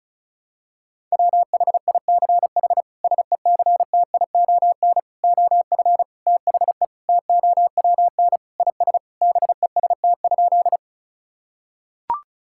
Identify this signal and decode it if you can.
Morse code, heard as a keyed tone, about 34 wpm, 700 Hz. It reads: WHICH SECTION OF THE TOWN IS BEST?
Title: Sentences from Top 1000 Words - Review Part 5 - 35wpm